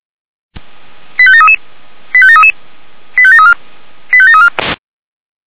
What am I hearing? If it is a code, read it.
Five-tone selective call (ZVEI), heard as quick runs of five tones, tones of about 70 ms. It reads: 85733 85733 85734 85734